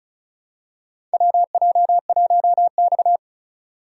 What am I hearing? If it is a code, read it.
Morse code, heard as a keyed tone, about 36 wpm, 700 Hz. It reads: WJ1X
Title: USA Call Signs - 4 Characters - 4x Character Spacing - 35wpm